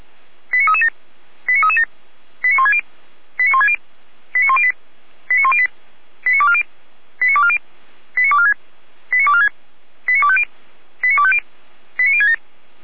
Five-tone selective call (ZVEI), heard as quick runs of five tones, tones of about 70 ms. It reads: 89337 89337 89160 89160 89108 89108 89240 89240 89246 89246 89250 89250 89067
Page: SDR/ZVEI